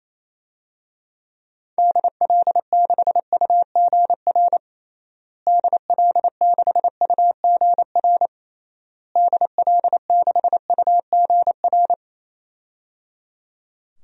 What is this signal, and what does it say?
Morse code, heard as a keyed tone, about 28 wpm, 700 Hz. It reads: DL6UGR DL6UGR DL6UGR
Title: DXCC-100 Call Signs - 6 Characters - Repeated 3x - 28wpm